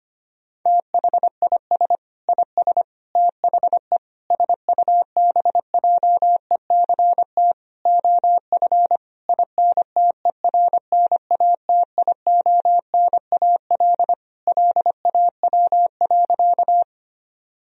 Morse code, heard as a keyed tone, about 25 wpm, 700 Hz. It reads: THIS IS THE SUBJECT OF INTERNATIONAL LAW.
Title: Sentences from Top 1000 Words - Review Part 4 - 25wpm